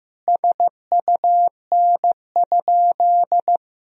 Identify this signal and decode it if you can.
Morse code, heard as a keyed tone, about 16 wpm, 700 Hz. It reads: SUN?
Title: Sentences from Top 1000 Words - Review Part 1 - 15wpm